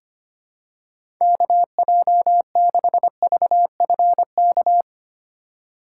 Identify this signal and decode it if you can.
Morse code, heard as a keyed tone, about 25 wpm, 700 Hz. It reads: KJ6VFK
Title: USA Call Signs - 6 Characters - 3x Character Spacing - 25wpm